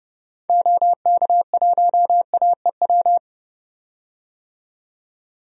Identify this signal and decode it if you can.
Morse code, heard as a keyed tone, about 31 wpm, 700 Hz. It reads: OK1AEW
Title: DXCC-100 Call Signs - 6 Characters - 30wpm